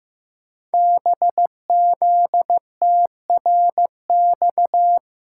Morse code, heard as a keyed tone, about 15 wpm, 700 Hz. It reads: BZTRX
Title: Code Groups - 5 Letter:Numbers - 15wpm